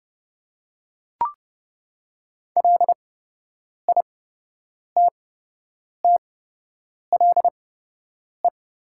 Morse code, heard as a keyed tone, about 30 wpm, spaced as for 8 wpm, 690 Hz. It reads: LITTLE
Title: 8x Character Spacing - Top 500 Words - 30wpm